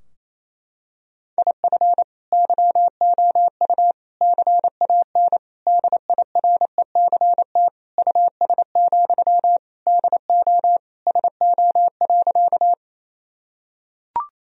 Morse code, heard as a keyed tone, about 28 wpm, 700 Hz. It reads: IF YOU CAN DIRECT US, DO SO.
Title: Sentences from Top 1000 Words - Review Part 2 - 28wpm